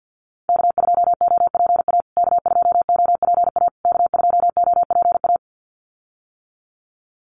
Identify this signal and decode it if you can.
Morse code, heard as a keyed tone, about 50 wpm, 700 Hz. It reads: K2OPA K2OPA K2OPA
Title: USA Call Signs Repeated 3x 50wpm